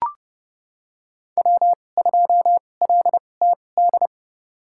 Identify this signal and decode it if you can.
Morse code, heard as a keyed tone, about 30 wpm, spaced as for 22 wpm, 700 Hz. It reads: W2LTD
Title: USA Call Signs - 2x Character Spacing - 30wpm